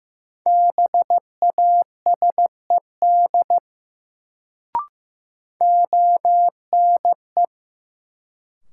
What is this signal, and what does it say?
Morse code, heard as a keyed tone, about 15 wpm, 700 Hz. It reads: BASED ONE
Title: Sets of 2 Words - New Word Drill - 15wpm